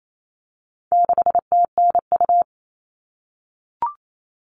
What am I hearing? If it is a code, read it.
Morse code, heard as a keyed tone, about 28 wpm, 700 Hz. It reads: 6TNU
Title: All Cut Numbers - 4-Digit 28wpm